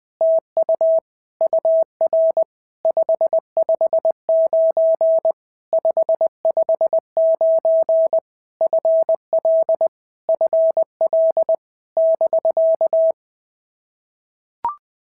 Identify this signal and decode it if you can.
Morse code, heard as a keyed tone, about 20 wpm, 650 Hz. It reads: TU UR 559 559 FL FL <BK>
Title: POTA QSO Practice - 20wpm